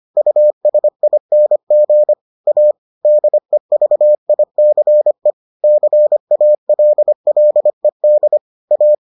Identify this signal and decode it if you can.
Morse code, heard as a keyed tone, about 25 wpm, 600 Hz. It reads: USING A DEVICE CALLED A